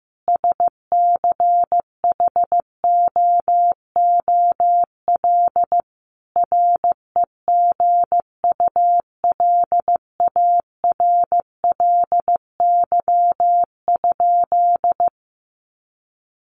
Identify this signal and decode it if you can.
Morse code, heard as a keyed tone, about 15 wpm, 700 Hz. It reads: SCHOOL REGULARLY?
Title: Sentences from Top 1400 Words 15wpm